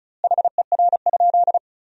Morse code, heard as a keyed tone, about 36 wpm, 700 Hz. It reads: HER?
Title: 2x Word-Spacing - Sentences from Top 200 Words - 35wpm